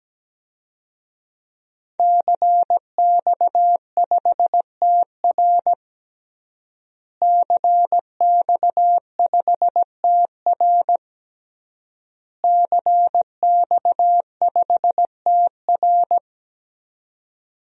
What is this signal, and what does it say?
Morse code, heard as a keyed tone, about 17 wpm, 700 Hz. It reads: CX5TR CX5TR CX5TR
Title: DXCC-100 Call Signs - 5 Characters - Repeated 3x - 17wpm